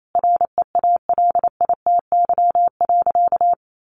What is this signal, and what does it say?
Morse code, heard as a keyed tone, about 28 wpm, 700 Hz. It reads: REALITY.
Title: Sentences from Top 1500 Words 28wpm